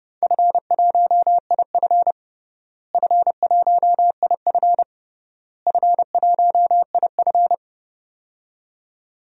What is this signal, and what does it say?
Morse code, heard as a keyed tone, about 30 wpm, 700 Hz. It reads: F1IF F1IF F1IF
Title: DXCC-100 Call Signs - 4 Characters - Repeated 3x - 30wpm